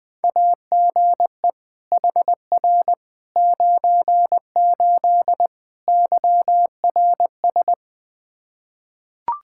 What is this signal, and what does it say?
Morse code, heard as a keyed tone, about 21 wpm, 700 Hz. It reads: AGE HR 98 YRS
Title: K1USN SST QSO Practice 20wpm